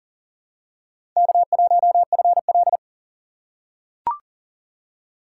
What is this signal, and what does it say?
Morse code, heard as a keyed tone, about 42 wpm, 700 Hz. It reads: K1FL
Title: USA Call Signs - 4 Characters - 4x Character Spacing - 40wpm